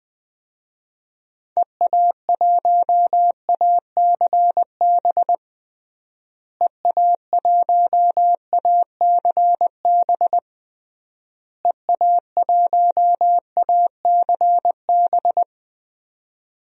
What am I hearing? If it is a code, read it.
Morse code, heard as a keyed tone, about 20 wpm, 700 Hz. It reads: EA1ACB EA1ACB EA1ACB
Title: DXCC-100 Call Signs - 6 Characters - Repeated 3x - 20wpm